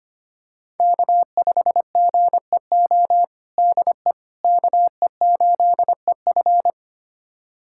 Morse code, heard as a keyed tone, about 25 wpm, 700 Hz. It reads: K5GEO DE KE8EF
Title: USA Callsign DE Callsign - 5 Characters - 25wpm